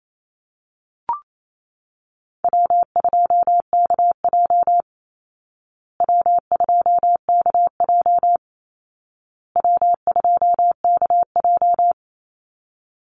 Morse code, heard as a keyed tone, about 28 wpm, 700 Hz. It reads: W2KJ W2KJ W2KJ
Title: Easy Call Signs Repeated 3x - 28wpm